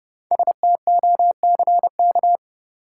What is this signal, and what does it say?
Morse code, heard as a keyed tone, about 30 wpm, 700 Hz. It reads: STOCK
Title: Sets of 5 Words - New Word Drill - 30wpm